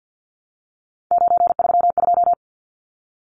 Morse code, heard as a keyed tone, about 50 wpm, 700 Hz. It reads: 932
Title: Numbers - 3-Digit 50wpm